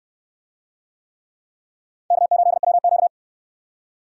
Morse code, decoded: K8WQ

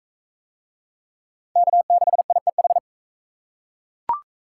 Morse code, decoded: K6IEH

